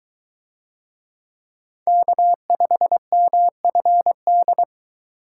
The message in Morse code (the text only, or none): K5MFD